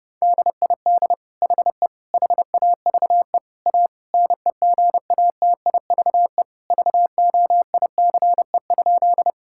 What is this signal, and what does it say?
Morse code, heard as a keyed tone, about 28 wpm, 700 Hz. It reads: DID HE HAVE A NEGATIVE VOICE?
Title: Sentences from Top 1000 Words - Review Part 6 - 30wpm